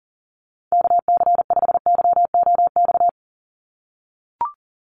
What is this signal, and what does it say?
Morse code, heard as a keyed tone, about 37 wpm, 700 Hz. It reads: KC5YOX
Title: USA Call Signs 40wpm